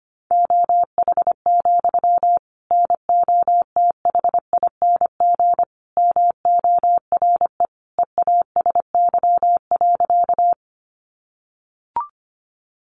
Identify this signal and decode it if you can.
Morse code, heard as a keyed tone, about 25 wpm, 700 Hz. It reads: OH, NOTHING MORE EASY.
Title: Sentences from Top 700 Words 25wpm